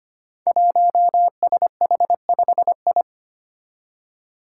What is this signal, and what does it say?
Morse code, heard as a keyed tone, about 25 wpm, 700 Hz. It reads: 1SH5I
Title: Five Challenging Letter-Number Combinations 25wpm